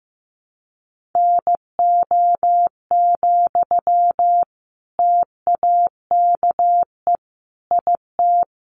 Morse code, heard as a keyed tone, about 15 wpm, 700 Hz. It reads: NO, TAKE IT
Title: Sentences from Top 600 Words 15wpm